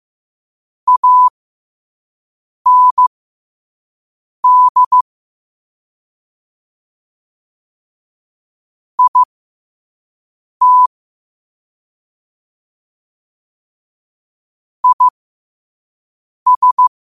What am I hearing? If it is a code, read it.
Morse code, heard as a keyed tone, about 15 wpm, spaced as for 5 wpm, 1000 Hz. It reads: AND IT IS
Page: New quotes every day in morse code at 5 Words per minute.